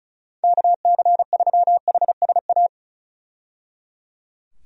Morse code, heard as a keyed tone, about 36 wpm, 700 Hz. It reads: KC3HSA
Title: USA Call Signs - 6 Characters - 35wpm